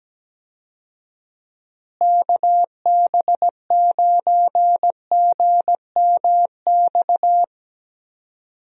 Morse code, heard as a keyed tone, about 17 wpm, 700 Hz. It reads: KB9GMX
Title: USA Call Signs - 6 Characters - Repeated 3x - 17wpm